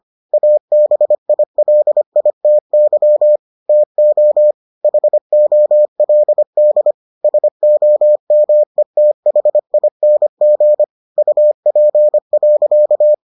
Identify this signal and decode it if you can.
Morse code, heard as a keyed tone, about 25 wpm, 600 Hz. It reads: ABILITY TO HOLD SOMETHING UP.